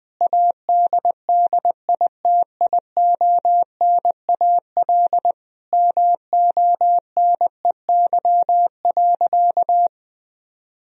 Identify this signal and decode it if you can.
Morse code, heard as a keyed tone, about 20 wpm, 700 Hz. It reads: ADDITIONAL MONEY.
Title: Sentences from Top 1000 Words - Review Part 4 - 20wpm